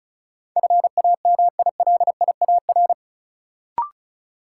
Morse code, heard as a keyed tone, about 35 wpm, 700 Hz. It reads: FAMILIAR
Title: Sentences from Top 1000 Words - New Word Drill - 35wpm